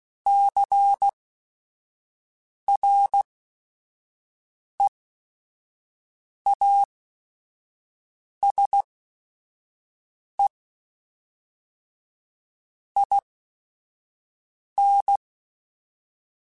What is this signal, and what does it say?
Morse code, heard as a keyed tone, about 16 wpm, spaced as for 5 wpm, 800 Hz. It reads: CREASE IN
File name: Technical Subjects / Technical Subjects-mp3s / XL @5